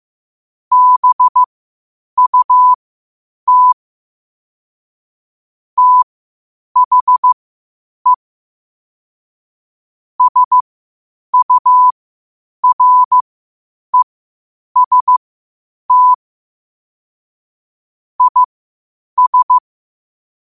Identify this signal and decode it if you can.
Morse code, heard as a keyed tone, about 15 wpm, spaced as for 8 wpm, 1000 Hz. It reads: BUT THE SUREST IS
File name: qotdfiles / 8wpm